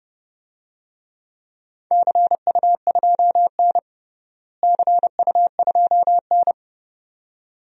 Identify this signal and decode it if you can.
Morse code, heard as a keyed tone, about 30 wpm, 700 Hz. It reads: CU2N CU2N
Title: DXCC-100 Call Signs - 4 Characters - Repeated 2x - 30wpm